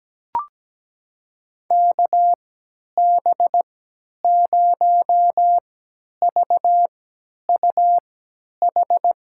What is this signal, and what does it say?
Morse code, heard as a keyed tone, about 17 wpm, spaced as for 10 wpm, 700 Hz. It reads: KB0VUH
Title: USA Call Signs - 6 Characters - 3x Character Spacing - 17wpm